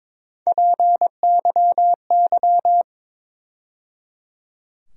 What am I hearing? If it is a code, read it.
Morse code, heard as a keyed tone, about 22 wpm, 700 Hz. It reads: PYY
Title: USA Call Sign Postfixes - 3 Letters 22wpm